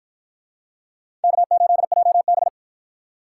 Morse code, heard as a keyed tone, about 53 wpm, 700 Hz. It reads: K8JB